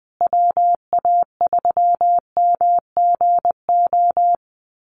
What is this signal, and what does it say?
Morse code, heard as a keyed tone, about 19 wpm, 700 Hz. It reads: WA3MGO
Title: USA Call Signs Repeated 3x 20wpm